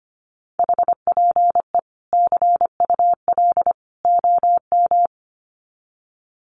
Morse code, heard as a keyed tone, about 25 wpm, 700 Hz. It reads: HPE CUL OM
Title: Common QSO Phrases 25wpm